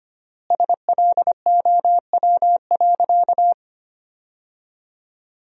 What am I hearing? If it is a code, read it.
Morse code, heard as a keyed tone, about 25 wpm, 700 Hz. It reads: SLOW.
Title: 3x Word-Spacing - Sentences from Top 400 Words - 25wpm